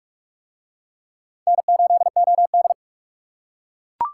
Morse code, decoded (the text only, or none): N8OD